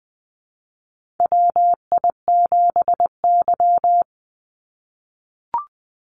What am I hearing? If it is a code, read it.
Morse code, heard as a keyed tone, about 20 wpm, 700 Hz. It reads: WI7Y
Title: USA Call Signs 20wpm